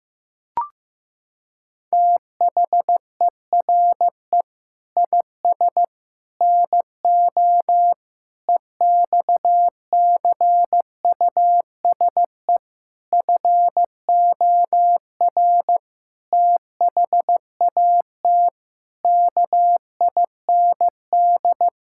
Morse code, heard as a keyed tone, about 15 wpm, 700 Hz. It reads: THERE IS NO EXCUSE FOR THAT KIND
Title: Sentences from Top 1000 Words - Review Part 6 - 15wpm